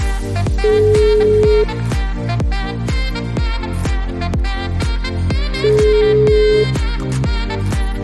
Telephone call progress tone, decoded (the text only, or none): ringback tone